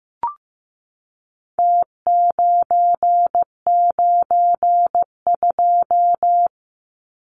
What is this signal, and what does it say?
Morse code, heard as a keyed tone, about 15 wpm, 700 Hz. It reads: T992
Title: Numbers - Contest Style - 4-Digit 15wpm